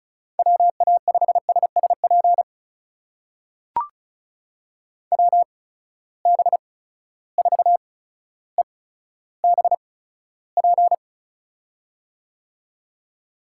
Morse code, heard as a keyed tone, about 34 wpm, spaced as for 15 wpm, 700 Hz. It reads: WA5SSP WB4EBP
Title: USA Call Signs - 6 Characters - 8x Character Spacing - 35wpm